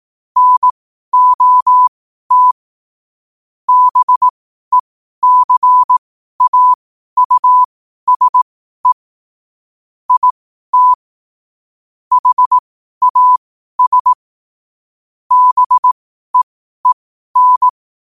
Morse code, heard as a keyed tone, about 18 wpm, spaced as for 12 wpm, 1000 Hz. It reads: NOT BECAUSE IT HAS BEEN
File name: qotdfiles / 12wpm